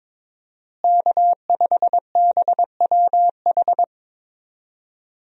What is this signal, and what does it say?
Morse code, heard as a keyed tone, about 22 wpm, 700 Hz. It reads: K5BWH
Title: USA Call Signs - 5 Characters - Repeated 2x - 22wpm